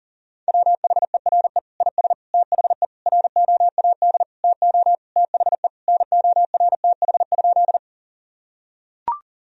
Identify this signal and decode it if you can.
Morse code, heard as a keyed tone, about 40 wpm, 700 Hz. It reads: WHERE IS THE ROAD TO THE NORTH?
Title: Sentences from Top 1000 Words - Review Part 2 - 40wpm